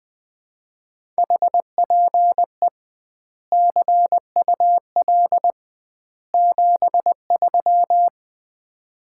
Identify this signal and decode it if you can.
Morse code, heard as a keyed tone, about 20 wpm, 700 Hz. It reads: HPE CUL 73
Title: Common QSO Phrases - 2x Word-Spacing - 20wpm